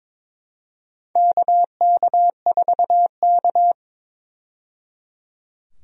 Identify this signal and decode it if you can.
Morse code, heard as a keyed tone, about 22 wpm, 700 Hz. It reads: KK4K
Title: USA Call Signs - 4 Characters - 22wpm